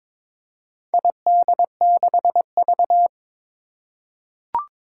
Morse code, heard as a keyed tone, about 22 wpm, 700 Hz. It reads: ID6V
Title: Four Challenging Letter-Number Combinations 22wpm